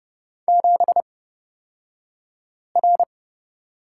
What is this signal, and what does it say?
Morse code, heard as a keyed tone, about 30 wpm, spaced as for 5 wpm, 700 Hz. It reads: 7R
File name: Lesson 38d - Letter q - Two Characters 30wpm